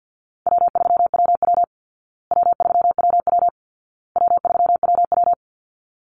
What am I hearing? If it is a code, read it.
Morse code, heard as a keyed tone, about 50 wpm, 700 Hz. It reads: W3WW W3WW W3WW
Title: Call Signs Repeated 2x - 50wpm